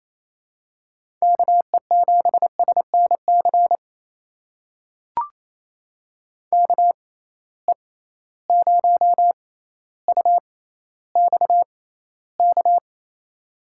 Morse code, heard as a keyed tone, about 28 wpm, spaced as for 14 wpm, 700 Hz. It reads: KE7SNC KE0UXK
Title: USA Call Signs - 6 Characters - 6x Character Spacing - 28wpm